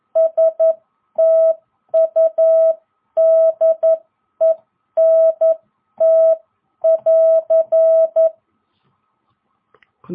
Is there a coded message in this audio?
Yes: Morse code from 0.2 to 8.3 s, heard as a keyed tone, about 11 wpm, spaced as for 9 wpm, 660 Hz. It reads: STUDENT<AR>